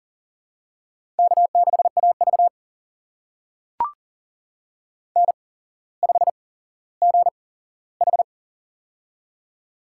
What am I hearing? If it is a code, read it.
Morse code, heard as a keyed tone, about 39 wpm, spaced as for 17 wpm, 700 Hz. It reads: K6AV N5GH
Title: USA Call Signs - 4 Characters - 8x Character Spacing - 40wpm